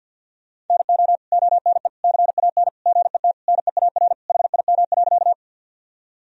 Morse code, heard as a keyed tone, about 50 wpm, 700 Hz. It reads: NO ONE CAN GET NEAR HIM.
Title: Sentences from Top 1000 Words - Review Part 1 - 50wpm